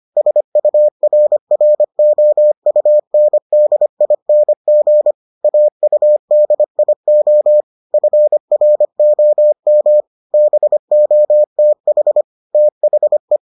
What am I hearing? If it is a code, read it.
Morse code, heard as a keyed tone, about 25 wpm, 600 Hz. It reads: SURROUNDING AUDIO FROM BOTH THE